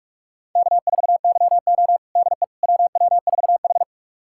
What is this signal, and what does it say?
Morse code, heard as a keyed tone, about 42 wpm, 700 Hz. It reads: K4YO DE WW4H
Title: USA Callsign DE Callsign - 4 Characters - 45wpm